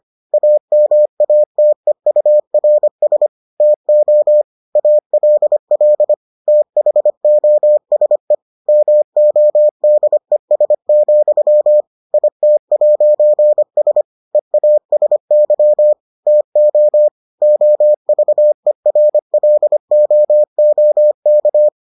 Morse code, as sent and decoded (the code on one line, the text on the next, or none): .- -- .- - . ..- .-. ... / - --- / .- .-.. .-.. / - .... --- ... / -- --- -.. ... --..-- / .. - .----. ... / . .- ... -.-- / - --- / --- ...- . .-. .-.. --- --- -.-
AMATEURS TO ALL THOSE MODES, IT'S EASY TO OVERLOOK